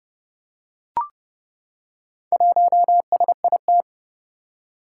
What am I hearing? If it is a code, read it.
Morse code, heard as a keyed tone, about 30 wpm, 700 Hz. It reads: 1SIT
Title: Four Challenging Letter-Number Combinations 30wpm